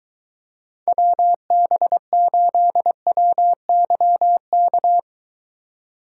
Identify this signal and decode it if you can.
Morse code, heard as a keyed tone, about 23 wpm, 700 Hz. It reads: WB8WYK